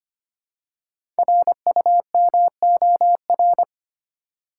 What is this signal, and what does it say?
Morse code, heard as a keyed tone, about 25 wpm, 700 Hz. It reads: RUMOR
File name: 5 Letter Words - First 3 Letters Given - 25wpm